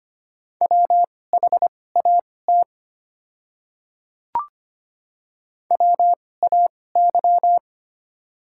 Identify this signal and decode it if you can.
Morse code, heard as a keyed tone, about 25 wpm, spaced as for 18 wpm, 690 Hz. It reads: WHAT WAY
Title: 2x Character Spacing - Top 200 Words - 25wpm